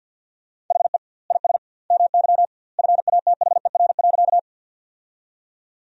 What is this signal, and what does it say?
Morse code, heard as a keyed tone, about 46 wpm, 700 Hz. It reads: HE IS MY FATHER.
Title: 2x Word-Spacing - Sentences from Top 300 Words - 50wpm